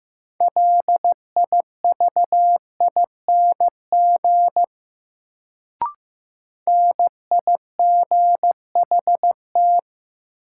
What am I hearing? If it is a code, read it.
Morse code, heard as a keyed tone, about 15 wpm, 700 Hz. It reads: LIVING NIGHT
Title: Sets of 4 Words - New Word Drill - 15wpm